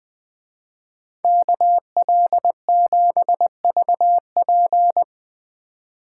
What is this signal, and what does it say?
Morse code, heard as a keyed tone, about 20 wpm, 700 Hz. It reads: KL7VP